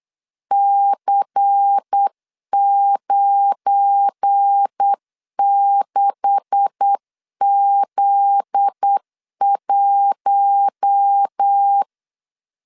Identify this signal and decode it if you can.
Morse code, heard as a keyed tone, about 8 wpm, 800 Hz. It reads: C96Z1
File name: LEZIONI / 9 286 8wpm 3425